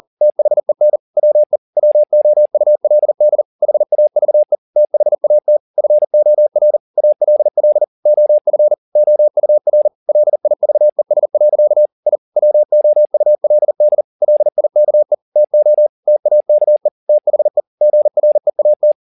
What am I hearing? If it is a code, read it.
Morse code, heard as a keyed tone, about 40 wpm, 600 Hz. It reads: THEN WE WOULD HAVE THAT FOR ALL OF OUR LIVES. I WOULD LIKE TO TAKE THE GREAT